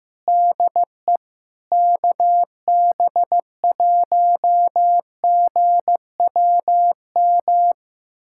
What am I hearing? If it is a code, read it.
Morse code, heard as a keyed tone, about 15 wpm, 700 Hz. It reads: DE KB1GWM
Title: USA Callsign DE Callsign - 6 Characters - 15wpm